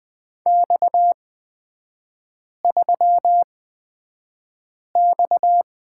Morse code, heard as a keyed tone, about 20 wpm, spaced as for 5 wpm, 700 Hz. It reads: X3X
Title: 229 Lesson 37e - Letter x - Three Characters 20wpm